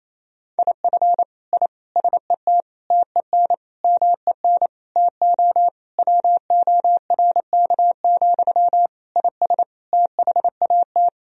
Morse code, heard as a keyed tone, about 28 wpm, 700 Hz. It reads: IF I SET TEN MEN TO WORK, IS THAT
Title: Sentences from Top 1000 Words - Review Part 2 - 28wpm